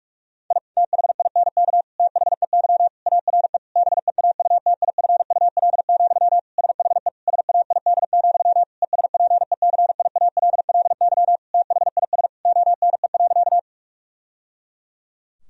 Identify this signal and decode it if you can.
Morse code, heard as a keyed tone, about 41 wpm, 700 Hz. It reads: I THINK THEY ARE BEAUTIFUL, SHE SAID, ESPECIALLY THIS ONE.
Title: Sentences from Top 1000 Words - Review Part 4 - 45wpm